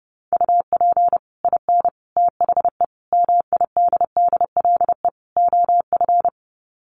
Morse code, heard as a keyed tone, about 30 wpm, 700 Hz. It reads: UP IN THE MIDDLE OF
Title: Sets of 5 Words Encore 30wpm